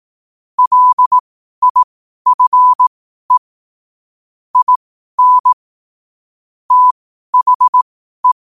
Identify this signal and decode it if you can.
Morse code, heard as a keyed tone, about 18 wpm, spaced as for 12 wpm, 1000 Hz. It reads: LIFE IN THE